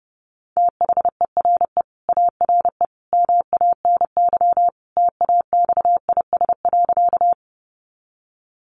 Morse code, heard as a keyed tone, about 30 wpm, 700 Hz. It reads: THERE ARE MANY TAXIS.
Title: Easy Phrases 30wpm